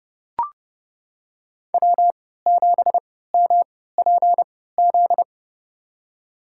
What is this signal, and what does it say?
Morse code, heard as a keyed tone, about 31 wpm, spaced as for 17 wpm, 700 Hz. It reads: W7MPZ
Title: USA Call Signs - 5 Characters - 3x Character Spacing - 30wpm